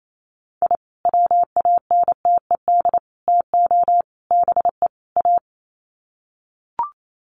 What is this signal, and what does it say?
Morse code, heard as a keyed tone, about 28 wpm, 700 Hz. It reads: I WANTED TO BE A
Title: Sets of 5 Words Encore2 28wpm